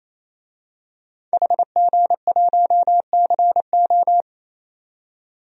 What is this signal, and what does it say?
Morse code, heard as a keyed tone, about 28 wpm, 700 Hz. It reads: HG1CO
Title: DXCC-100 Call Signs - 5 Characters - 28wpm